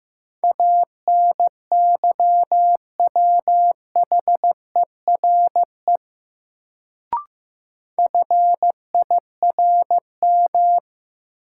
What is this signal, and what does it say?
Morse code, heard as a keyed tone, about 15 wpm, 700 Hz. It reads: ANYWHERE FIRM
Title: Sentences from Top 1400 Words - New Word Drill - 15wpm